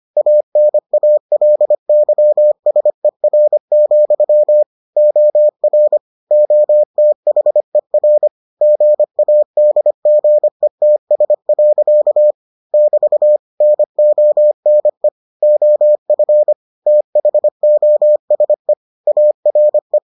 Morse code, heard as a keyed tone, about 25 wpm, 600 Hz. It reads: ANALYSER, OR OTHER GADGETS. <BT> NONE OF THOSE ARE